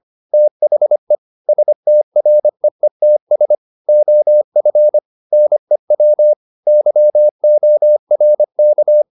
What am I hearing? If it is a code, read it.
Morse code, heard as a keyed tone, about 25 wpm, 600 Hz. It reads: THE STREETS OF NEW YORK